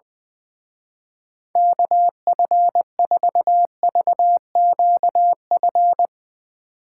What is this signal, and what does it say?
Morse code, heard as a keyed tone, about 20 wpm, 700 Hz. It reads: KF4VQF
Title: Speed Racing - USA 2x3 Call Signs 20wpm